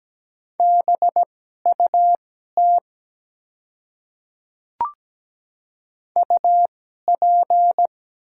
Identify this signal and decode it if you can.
Morse code, heard as a keyed tone, about 17 wpm, spaced as for 12 wpm, 700 Hz. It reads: BUT UP
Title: 2x Character Spacing - Top 100 Words - 17wpm